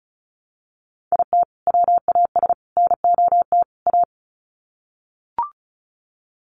Morse code, heard as a keyed tone, about 34 wpm, 700 Hz. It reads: IT WAS NOT A
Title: Sets of 4 Words 35wpm